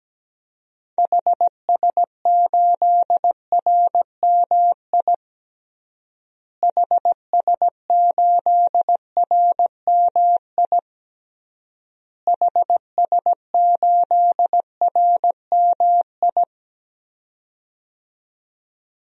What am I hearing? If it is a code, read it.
Morse code, heard as a keyed tone, about 17 wpm, 700 Hz. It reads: HS8RMI HS8RMI HS8RMI
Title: DXCC-100 Call Signs - 6 Characters - Repeated 3x - 17wpm